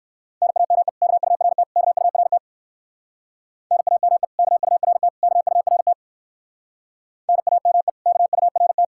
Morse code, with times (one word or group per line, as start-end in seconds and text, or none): 0.4-0.9 NAME
1.0-1.6 KURT
1.8-2.4 KURT
3.7-4.2 NAME
4.4-5.1 KURT
5.2-5.9 KURT
7.3-7.9 NAME
8.1-8.8 KURT